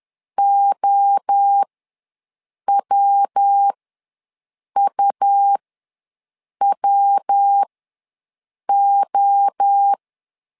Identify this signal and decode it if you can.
Morse code, heard as a keyed tone, about 10 wpm, spaced as for 6 wpm, 800 Hz. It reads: OWUWO
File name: LEZIONI / 3 MWRO / 3 MWRO ws 10wpm 2310